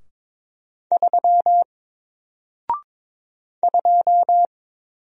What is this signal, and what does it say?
Morse code, heard as a keyed tone, about 22 wpm, 700 Hz. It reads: 3 2
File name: Single-Letter-Number-22wpm